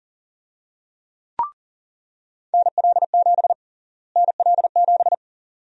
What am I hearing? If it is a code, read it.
Morse code, heard as a keyed tone, about 40 wpm, 700 Hz. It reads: NL7 NL7
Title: USA Call Sign Prefixes - Repeated 2x - 40wpm